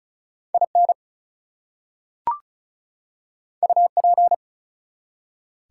Morse code, heard as a keyed tone, about 36 wpm, 700 Hz. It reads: IN UP
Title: Two Letter Words - 35wpm